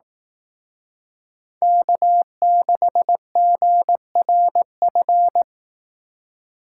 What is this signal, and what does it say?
Morse code, heard as a keyed tone, about 18 wpm, 700 Hz. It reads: K6GRF
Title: Speed Racing - USA Call Signs 15wpm